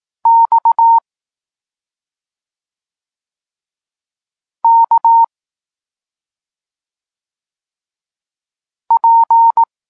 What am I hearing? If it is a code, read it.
Morse code, heard as a keyed tone, about 18 wpm, 920 Hz. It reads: X K P
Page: Tekst 32 – Doskonalenie odbioru poznanych liter